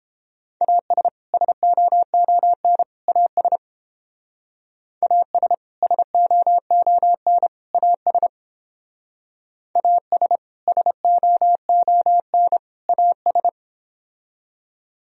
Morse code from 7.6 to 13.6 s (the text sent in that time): AS AS SOON AS